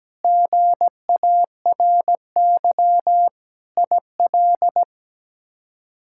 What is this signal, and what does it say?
Morse code, heard as a keyed tone, about 17 wpm, 700 Hz. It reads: GARY IL
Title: K1USN SST QSO Practice - 17wpm